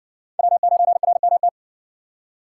Morse code, heard as a keyed tone, about 60 wpm, 700 Hz. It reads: W9RMT